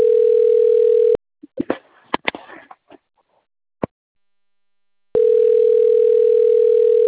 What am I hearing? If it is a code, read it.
Ringback tone (North America), 440 + 480 Hz.